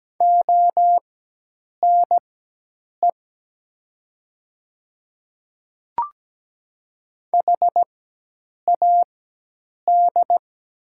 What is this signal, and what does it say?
Morse code, heard as a keyed tone, about 17 wpm, spaced as for 8 wpm, 700 Hz. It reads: ONE HAD
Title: 4x Character Spacing - Top 100 Words - 17wpm